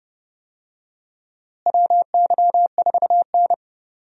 Morse code, heard as a keyed tone, about 30 wpm, 700 Hz. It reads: WY4N
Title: USA Call Signs - 4 Characters - Repeated 2x - 30wpm